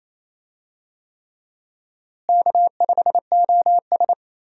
Morse code, heard as a keyed tone, about 28 wpm, 700 Hz. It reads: K5OS
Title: USA Call Signs - 4 Characters - Repeated 3x - 28wpm